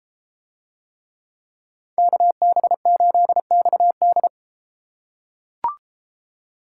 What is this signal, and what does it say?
Morse code, heard as a keyed tone, about 31 wpm, 700 Hz. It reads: KB8XD